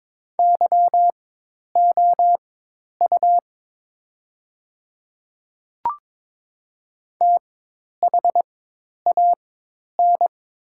Morse code, heard as a keyed tone, about 22 wpm, spaced as for 10 wpm, 700 Hz. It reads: YOU THAN
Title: 4x Character Spacing - Top 100 Words - 22wpm